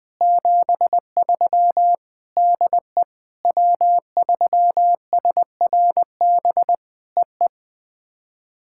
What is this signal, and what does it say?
Morse code, heard as a keyed tone, about 20 wpm, 700 Hz. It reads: 73 DE W3SRB EE
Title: POTA QSO Practice - 20wpm